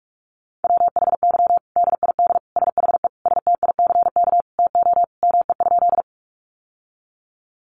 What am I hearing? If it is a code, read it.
Morse code, heard as a keyed tone, about 44 wpm, 700 Hz. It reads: WHY DID SHE STICK TO ME?
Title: Sentences from Top 1200 Words 45wpm